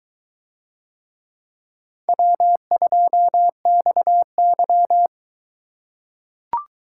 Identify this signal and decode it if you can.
Morse code, heard as a keyed tone, about 22 wpm, 700 Hz. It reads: W2XY